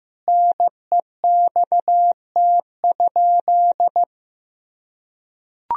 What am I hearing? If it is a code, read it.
Morse code, heard as a keyed tone, about 15 wpm, 700 Hz. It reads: NEXT?
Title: Sentences from Top 1000 Words - Review Part 1 - 15wpm